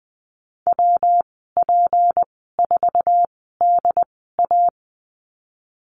Morse code, heard as a keyed tone, about 20 wpm, spaced as for 15 wpm, 700 Hz. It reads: WP4DA
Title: 167 Lesson 27g - Number 7 - Call Signs 20wpm